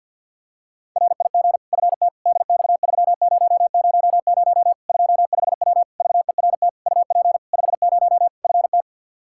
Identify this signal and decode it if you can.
Morse code, heard as a keyed tone, about 47 wpm, 700 Hz. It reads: RIG FT DX3000 25W VERT UP 50 FT
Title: Practice QSO - Exchange 2 - Info - 50wpm